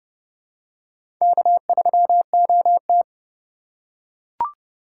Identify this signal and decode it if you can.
Morse code, heard as a keyed tone, about 30 wpm, 700 Hz. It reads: K3OT
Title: USA Call Signs - 4 Characters - 4x Character Spacing - 30wpm